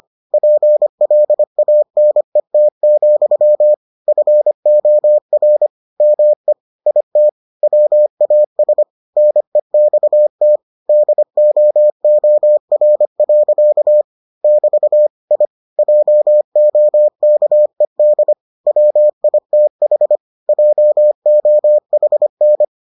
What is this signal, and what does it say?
Morse code, heard as a keyed tone, about 25 wpm, 600 Hz. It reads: PLANET, FOR ME IT WAS NEXT DOOR. <BT> I JOKED WITH JOHN